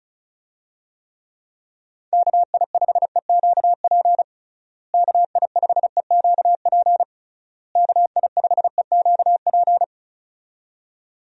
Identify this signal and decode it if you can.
Morse code, heard as a keyed tone, about 34 wpm, 700 Hz. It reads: KI5EQP KI5EQP KI5EQP
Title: USA Call Signs - 6 Characters - Repeated 3x - 35wpm